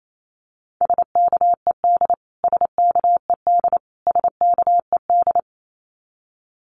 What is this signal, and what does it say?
Morse code, heard as a keyed tone, about 28 wpm, 700 Hz. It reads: SKED SKED SKED
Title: Abbreviations Repeated 3x - 28wpm